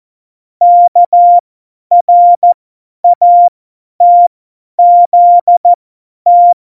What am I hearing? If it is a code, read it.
Morse code, heard as a keyed tone, about 14 wpm, spaced as for 10 wpm, 700 Hz. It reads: KRATZT
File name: Morsekurs / Witze / Witz 18. September 2018, 50 F70